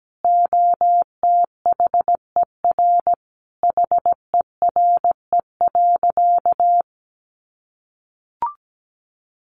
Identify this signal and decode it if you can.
Morse code, heard as a keyed tone, about 17 wpm, 700 Hz. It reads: OTHER HERE.
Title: Sentences from Top 1200 Words 17wpm